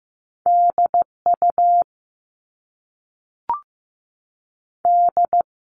Morse code, heard as a keyed tone, about 15 wpm, 700 Hz. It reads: D U D
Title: All Cut Numbers - 2-Digit 15wpm